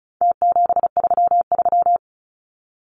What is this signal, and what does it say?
Morse code, heard as a keyed tone, about 34 wpm, 700 Hz. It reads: T733
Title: Numbers - Contest Style - 4-Digit 35wpm